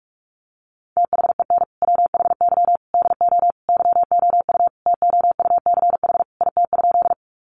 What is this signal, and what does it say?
Morse code, heard as a keyed tone, about 44 wpm, 700 Hz. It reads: THEN WHY DO YOU TOUCH IT?
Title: Sentences from Top 1200 Words 45wpm